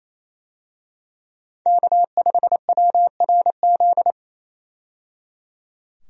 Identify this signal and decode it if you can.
Morse code, heard as a keyed tone, about 27 wpm, 700 Hz. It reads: K5WRZ
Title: USA Call Signs - 5 Characters - 28wpm